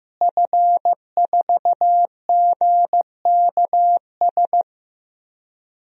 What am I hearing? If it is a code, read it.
Morse code, heard as a keyed tone, about 15 wpm, 700 Hz. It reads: F4GKS
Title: DXCC-100 Call Signs - 5 Characters - Repeated 2x - 15wpm